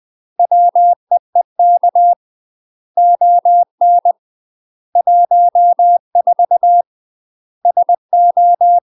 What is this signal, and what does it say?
Morse code, heard as a keyed tone, about 20 wpm, 700 Hz. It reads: WEEK ON 14 SO
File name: UK MASTER QSOs TRAINING / 20WPM / UK MASTER QSOs 0044